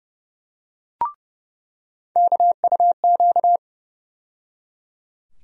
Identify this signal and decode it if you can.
Morse code, heard as a keyed tone, about 30 wpm, 700 Hz. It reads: KUQ
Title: USA Call Sign Postfixes - 3 Letters 30wpm